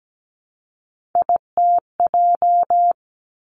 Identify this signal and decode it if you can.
Morse code, heard as a keyed tone, about 17 wpm, 700 Hz. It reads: ITJ
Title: Three Challenging Letter-Number Combinations 17wpm